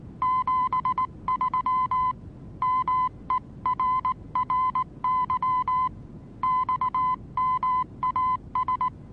Morse code, heard as a keyed tone, about 19 wpm, 1020 Hz. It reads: 73 MERRY XMAS
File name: repos / Morsekurs / DJ4UF Morsen26